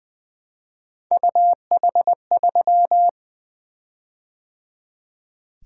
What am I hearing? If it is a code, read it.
Morse code, heard as a keyed tone, about 20 wpm, 700 Hz. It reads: UH3
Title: Three Letter-Number Combinations - 20wpm